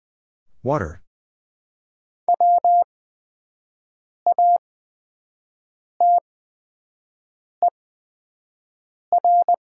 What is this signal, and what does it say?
Morse code, heard as a keyed tone, about 20 wpm, spaced as for 5 wpm, 700 Hz. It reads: E WATER
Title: 8x Character Spacing - Top 100 Words - 20wpm